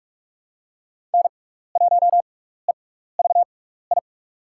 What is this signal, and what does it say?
Morse code, heard as a keyed tone, about 44 wpm, spaced as for 15 wpm, 700 Hz. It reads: N1EVI
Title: USA Call Signs - 6x Character Spacing - 45wpm